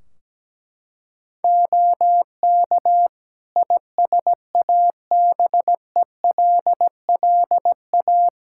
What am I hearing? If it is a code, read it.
Morse code, heard as a keyed tone, about 17 wpm, 700 Hz. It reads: OK ISABELLA
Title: Practice QSO - Exchange 2 - Info - 17wpm